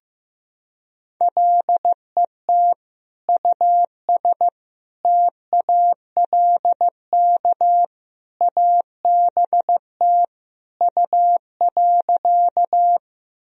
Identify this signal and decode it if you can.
Morse code, heard as a keyed tone, about 15 wpm, 700 Hz. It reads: LET US TALK ABT U.
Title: Sentences from Top 400 Words and Abbreviations 15wpm